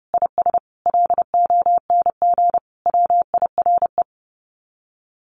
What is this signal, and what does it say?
Morse code, heard as a keyed tone, about 30 wpm, 700 Hz. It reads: IS LONG WIRE
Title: Common QSO Phrases 30wpm